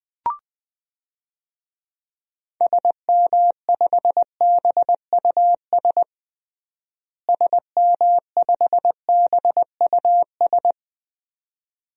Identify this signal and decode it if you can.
Morse code, heard as a keyed tone, about 21 wpm, 700 Hz. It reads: SM5BUS SM5BUS
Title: DXCC-100 Call Signs - 6 Characters - Repeated 2x - 20wpm